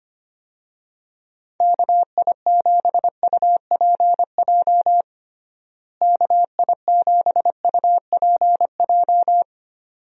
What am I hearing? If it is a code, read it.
Morse code, heard as a keyed tone, about 25 wpm, 700 Hz. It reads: KI7UPJ KI7UPJ
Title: USA Call Signs - 6 Characters - Repeated 2x - 25wpm